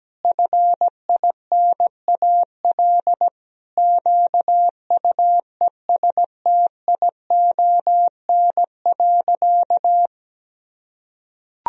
Morse code, heard as a keyed tone, about 17 wpm, 700 Hz. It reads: FINAL QUESTION.
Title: Sentences from Top 1000 Words - Review Part 2 - 17wpm